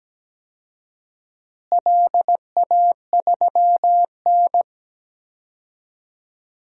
Morse code, heard as a keyed tone, about 17 wpm, 700 Hz. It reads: LA3N
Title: DXCC-100 Call Signs - 4 Characters - Repeated 2x - 17wpm